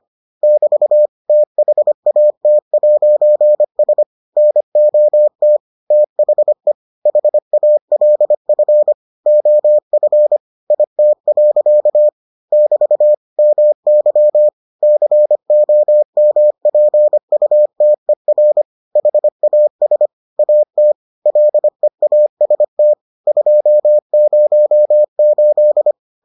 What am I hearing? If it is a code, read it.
Morse code, heard as a keyed tone, about 25 wpm, 600 Hz. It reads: <BT> THAT'S NOT THE HALF OF IT. <BT> MY COMPUTER HAS AT LEAST 208